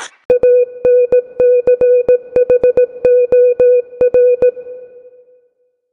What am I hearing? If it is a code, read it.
Morse code, heard as a keyed tone, about 17 wpm, 500 Hz. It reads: ANCHOR